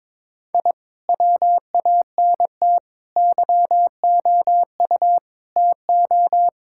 Morse code, heard as a keyed tone, about 21 wpm, 700 Hz. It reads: I WANT YOU TO
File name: practice / 22wpm / Sentences with 2-Word Phrases - 22wpm